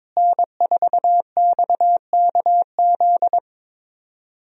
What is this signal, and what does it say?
Morse code, heard as a keyed tone, about 21 wpm, 700 Hz. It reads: N4XKZ
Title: USA Call Signs - 5 Characters - Repeated 2x - 22wpm